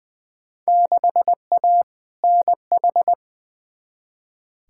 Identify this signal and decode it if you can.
Morse code, heard as a keyed tone, about 20 wpm, 700 Hz. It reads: 6A NH
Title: Field Day Exchanges - 20wpm